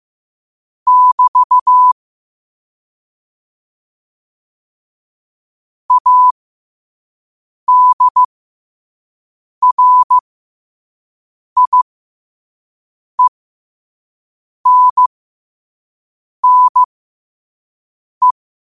Morse code, heard as a keyed tone, about 15 wpm, spaced as for 5 wpm, 1000 Hz. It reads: <BT> ADRIENNE
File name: qotdfiles / 5wpm